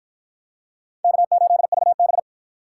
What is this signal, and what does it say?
Morse code, heard as a keyed tone, about 47 wpm, 700 Hz. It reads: K8VB